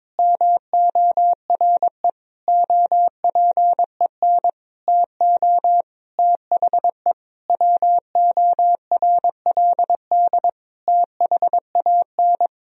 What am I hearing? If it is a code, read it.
Morse code, heard as a keyed tone, about 22 wpm, 700 Hz. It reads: MORE OPEN TO THE WORLD THAN
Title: Sentences from Top 1000 Words - Review Part 3 - 22wpm